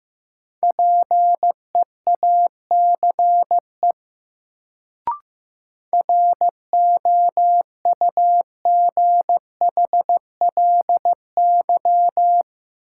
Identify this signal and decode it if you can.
Morse code, heard as a keyed tone, about 15 wpm, 700 Hz. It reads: PEACE ROUGHLY
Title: Sentences from Top 1500 Words - New Word Drill - 15wpm